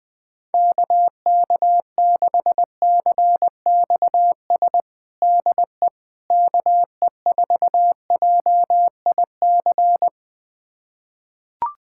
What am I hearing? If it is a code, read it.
Morse code, heard as a keyed tone, about 20 wpm, 700 Hz. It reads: KK6CXS DE KE4JIC
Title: USA Callsign DE Callsign - 6 Characters - 20wpm